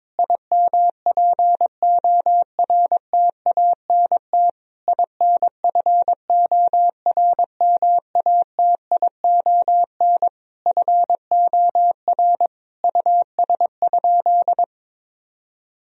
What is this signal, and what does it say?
Morse code, heard as a keyed tone, about 22 wpm, 700 Hz. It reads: IMPORTANT INFORMATION FOR US?
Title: Sentences from Top 1000 Words - Review Part 3 - 22wpm